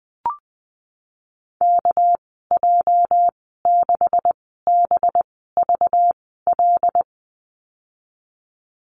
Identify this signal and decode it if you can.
Morse code, heard as a keyed tone, about 19 wpm, spaced as for 14 wpm, 700 Hz. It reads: KJ6BVL
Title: 219 Lesson 35g - Number 0 - Call Signs 20wpm